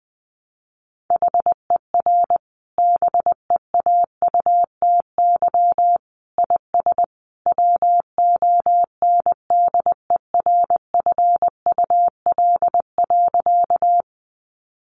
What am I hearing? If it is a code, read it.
Morse code, heard as a keyed tone, about 20 wpm, 700 Hz. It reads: HER BEAUTY IS WONDERFUL.
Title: Sentences from Top 1100 Words 20wpm